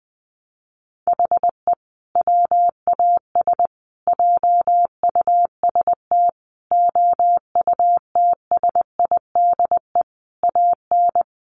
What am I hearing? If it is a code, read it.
Morse code, heard as a keyed tone, about 20 wpm, 700 Hz. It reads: HE WAS JUST OUTSIDE AN
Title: Sentences from Top 1100 Words 20wpm